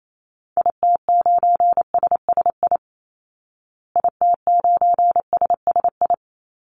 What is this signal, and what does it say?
Morse code, heard as a keyed tone, about 28 wpm, 700 Hz. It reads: IT9SSI IT9SSI
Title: Call Signs Repeated 3x - 28wpm